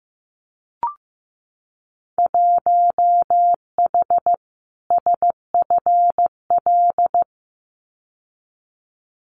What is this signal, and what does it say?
Morse code, heard as a keyed tone, about 15 wpm, 700 Hz. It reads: 1H SFL
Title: Winter Field Day Exchanges 15wpm